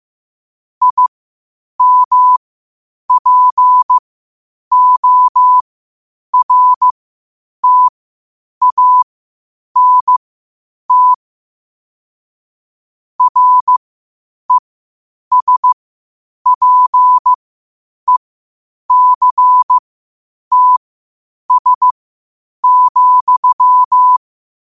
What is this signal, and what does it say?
Morse code, heard as a keyed tone, about 15 wpm, spaced as for 8 wpm, 1000 Hz. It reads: IMPORTANT RESPECTS,
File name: qotdfiles / 8wpm